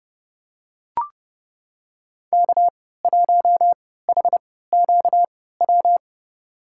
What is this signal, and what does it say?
Morse code, heard as a keyed tone, about 30 wpm, spaced as for 17 wpm, 700 Hz. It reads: K1HQW
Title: USA Call Signs - 5 Characters - 3x Character Spacing - 30wpm